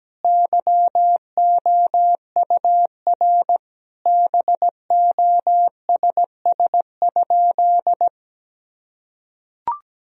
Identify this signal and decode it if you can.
Morse code, heard as a keyed tone, about 17 wpm, 700 Hz. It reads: YOUR BOSS?
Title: Sentences from Top 1000 Words - Review Part 4 - 17wpm